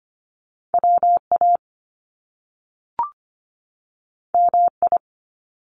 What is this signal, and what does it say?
Morse code, heard as a keyed tone, about 25 wpm, 700 Hz. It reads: WA MI
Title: US States 25wpm